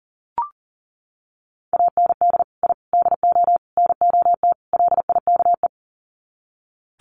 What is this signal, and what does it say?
Morse code, heard as a keyed tone, about 40 wpm, 700 Hz. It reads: AND I DO NOT LIKE
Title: Sets of 5 Words Encore2 40wpm